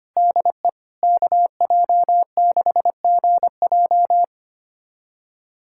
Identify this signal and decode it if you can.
Morse code, heard as a keyed tone, about 25 wpm, 700 Hz. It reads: DE KJ6GJ
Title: USA Callsign DE Callsign 25wpm